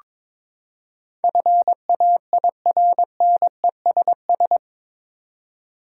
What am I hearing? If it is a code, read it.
Morse code, heard as a keyed tone, about 22 wpm, 700 Hz. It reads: FAIRNESS
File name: practice / 22wpm / Eight Letter Words - 22wpm